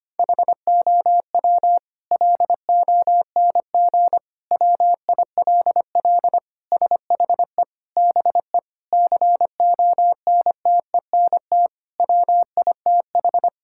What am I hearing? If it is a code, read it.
Morse code, heard as a keyed tone, about 25 wpm, 700 Hz. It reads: HOW LONG WILL SHE BE CONTENT WITH
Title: Sentences from Top 1000 Words - Review Part 6 - 25wpm